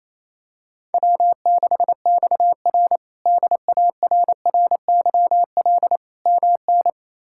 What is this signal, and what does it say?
Morse code, heard as a keyed tone, about 28 wpm, 700 Hz. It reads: W6XR DARRYL MN
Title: K1USN SST QSO Practice - 28wpm